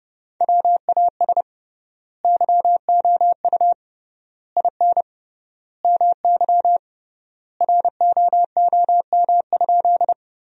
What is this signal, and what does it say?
Morse code, heard as a keyed tone, about 30 wpm, 700 Hz. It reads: WAS YOU IN MY ROOM?
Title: 3x Word-Spacing - Sentences from Top 300 Words - 30wpm